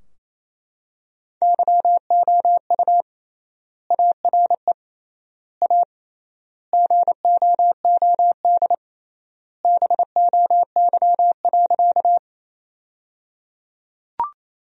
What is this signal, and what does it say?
Morse code, heard as a keyed tone, about 27 wpm, 700 Hz. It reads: YOU ARE A GOOD BOY.
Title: 3x Word-Spacing - Sentences from Top 200 Words - 28wpm